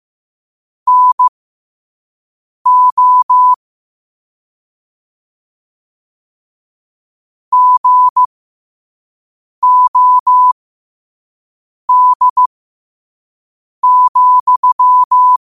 Morse code, heard as a keyed tone, about 15 wpm, spaced as for 5 wpm, 1000 Hz. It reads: NO GOD,
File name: qotdfiles / 5wpm